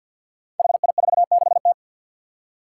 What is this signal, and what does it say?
Morse code, heard as a keyed tone, about 48 wpm, 700 Hz. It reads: HI46T
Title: Five Challenging Letter-Number Combinations 50wpm